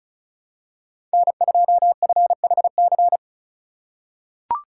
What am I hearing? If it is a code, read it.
Morse code, heard as a keyed tone, about 34 wpm, 700 Hz. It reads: N2FHC
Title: USA Call Signs - 5 Characters - 3x Character Spacing - 35wpm